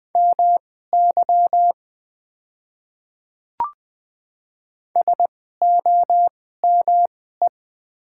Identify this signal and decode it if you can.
Morse code, heard as a keyed tone, about 20 wpm, spaced as for 14 wpm, 700 Hz. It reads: MY SOME
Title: 2x Character Spacing - Top 200 Words - 20wpm